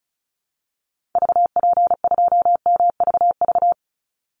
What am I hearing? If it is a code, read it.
Morse code, heard as a keyed tone, about 34 wpm, 700 Hz. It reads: VP2MVV
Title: Call Signs Repeated 3x - 35wpm